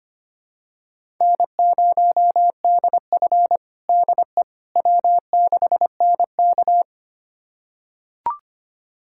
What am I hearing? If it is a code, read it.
Morse code, heard as a keyed tone, about 25 wpm, 700 Hz. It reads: N0DF DE W6NK
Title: USA Callsign DE Callsign - 4 Characters - 25wpm